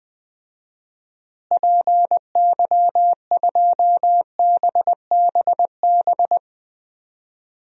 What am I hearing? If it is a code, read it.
Morse code, heard as a keyed tone, about 20 wpm, 700 Hz. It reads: PY2BBB
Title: DXCC-100 Call Signs - 6 Characters - Repeated 2x - 20wpm